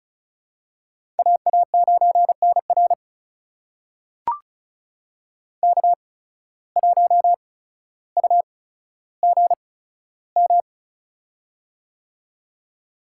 Morse code, heard as a keyed tone, about 36 wpm, spaced as for 15 wpm, 700 Hz. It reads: AA9NR K1UGM
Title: USA Call Signs - 5 Characters - 8x Character Spacing - 35wpm